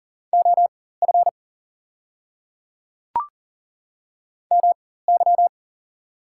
Morse code, heard as a keyed tone, about 40 wpm, spaced as for 19 wpm, 690 Hz. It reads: OF MY
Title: 4x Character Spacing - Top 300 Words - 40wpm